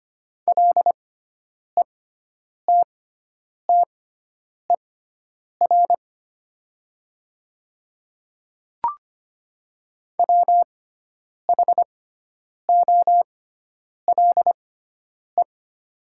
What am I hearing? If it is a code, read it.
Morse code, heard as a keyed tone, about 25 wpm, spaced as for 9 wpm, 700 Hz. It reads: LETTER WHOLE
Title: 6x Character Spacing - Top 500 Words - 25wpm